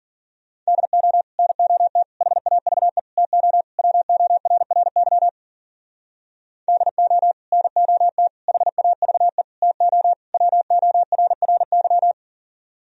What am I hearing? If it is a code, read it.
Morse code, heard as a keyed tone, about 51 wpm, 700 Hz. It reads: DO NOT HAVE TO WORRY DO NOT HAVE TO WORRY